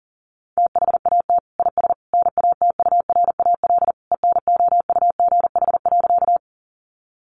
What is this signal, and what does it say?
Morse code, heard as a keyed tone, about 40 wpm, 700 Hz. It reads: THAT IS NATURAL ENOUGH.
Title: Sentences from Top 700 Words 40wpm